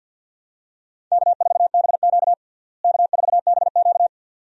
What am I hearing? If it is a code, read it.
Morse code, heard as a keyed tone, about 52 wpm, 700 Hz. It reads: K4BQ K4BQ
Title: USA Call Signs - 4 Characters - Repeated 2x - 50wpm